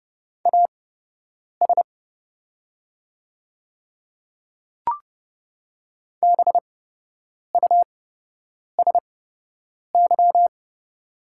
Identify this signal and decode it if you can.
Morse code, heard as a keyed tone, about 30 wpm, spaced as for 8 wpm, 700 Hz. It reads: AS BUSY
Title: 8x Character Spacing - Top 500 Words - 30wpm